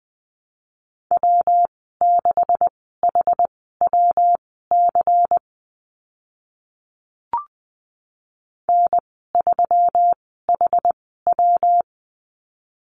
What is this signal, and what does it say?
Morse code, heard as a keyed tone, about 20 wpm, spaced as for 15 wpm, 700 Hz. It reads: W6HWC N3HW
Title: 140 Lesson 23g - Letter f - Call Signs 20wpm